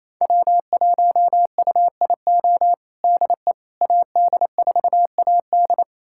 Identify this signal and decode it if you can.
Morse code, heard as a keyed tone, about 28 wpm, 700 Hz. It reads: W1UIO DE AD4AD
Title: USA Callsign DE Callsign - 5 Characters - 28wpm